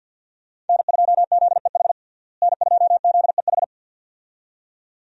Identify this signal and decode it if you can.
Morse code, heard as a keyed tone, about 53 wpm, 700 Hz. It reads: N2ZEH N2ZEH
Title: USA Call Signs - 5 Characters - Repeated 2x - 50wpm